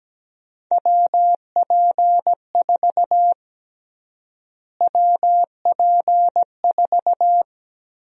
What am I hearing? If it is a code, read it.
Morse code, heard as a keyed tone, about 17 wpm, 700 Hz. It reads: WP4 WP4
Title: USA Call Sign Prefixes - Repeated 2x - 17wpm